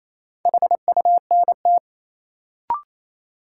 Morse code, heard as a keyed tone, about 27 wpm, 700 Hz. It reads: HUNT
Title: Sentences from Top 1600 Words - New Word Drill - 28wpm